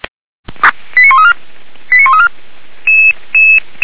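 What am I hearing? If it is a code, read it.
Five-tone selective call (ZVEI), heard as quick runs of five tones, tones of about 80 ms. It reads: 89134 89134
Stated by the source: Selective call received by SDR, logged by Feuerwehr Orlamünde.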